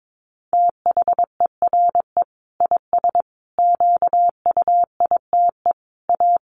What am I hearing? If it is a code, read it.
Morse code, heard as a keyed tone, about 22 wpm, 700 Hz. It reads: THERE IS QUITE A
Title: Sentences from Top 800 Words 22wpm